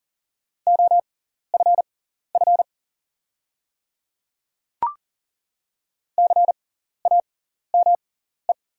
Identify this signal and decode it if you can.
Morse code, heard as a keyed tone, about 37 wpm, spaced as for 14 wpm, 690 Hz. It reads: OFF CAME
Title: 6x Character Spacing - Top 300 Words - 40wpm